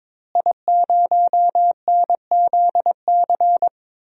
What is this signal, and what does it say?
Morse code, heard as a keyed tone, about 22 wpm, 700 Hz. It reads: I0NZC
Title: DXCC-100 Call Signs - 5 Characters - Repeated 3x - 22wpm